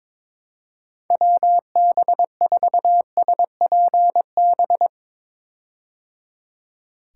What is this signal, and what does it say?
Morse code, heard as a keyed tone, about 22 wpm, 700 Hz. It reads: WB4SPB
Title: K1USN SST QSO Practice - 22wpm